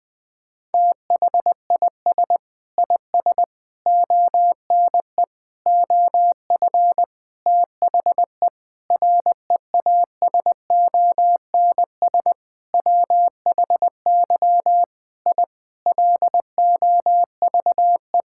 Morse code, heard as a keyed tone, about 20 wpm, 700 Hz. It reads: THIS IS ONE OF THE REASONS WHY I LOVE
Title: Sentences with 7-Word Phrases - 20wpm